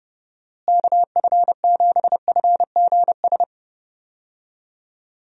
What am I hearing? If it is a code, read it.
Morse code, heard as a keyed tone, about 30 wpm, 700 Hz. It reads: KF7FGS
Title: USA Call Signs - 6 Characters - Repeated 3x - 30wpm